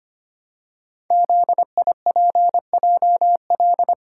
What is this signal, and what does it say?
Morse code, heard as a keyed tone, about 25 wpm, 700 Hz. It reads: ZIPJL
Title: Code Groups - 5 Letter:Numbers - 25wpm